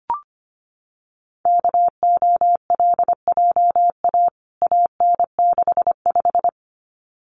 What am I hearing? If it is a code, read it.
Morse code, heard as a keyed tone, about 25 wpm, 700 Hz. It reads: KOLJA AN65
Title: CWT Exchange 25wpm